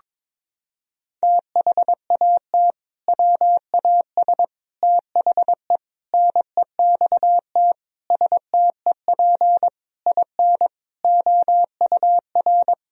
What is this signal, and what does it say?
Morse code, heard as a keyed tone, about 22 wpm, 700 Hz. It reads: THAT WAS THE NEXT STEP IN OUR
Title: Sentences from Top 1000 Words - Review Part 4 - 22wpm